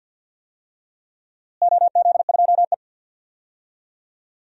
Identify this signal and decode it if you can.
Morse code, heard as a keyed tone, about 50 wpm, 700 Hz. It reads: OZ2E
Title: DXCC-100 Call Signs - 4 Characters - Repeated 2x - 50wpm